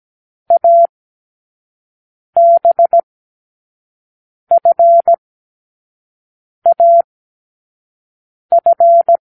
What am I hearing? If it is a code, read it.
Morse code, heard as a keyed tone, about 17 wpm, 680 Hz. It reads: A B F A F
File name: k7qo / Lesson12(A-H)